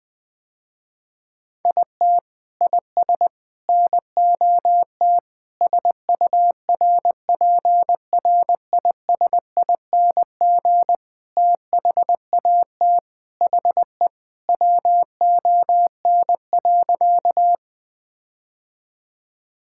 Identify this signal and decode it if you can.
Morse code, heard as a keyed tone, about 20 wpm, 700 Hz. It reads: IT IS NOT SURPRISING THAT HE WON.
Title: Sentences with 5-Word Phrases - Spoken Beginning - 20wpm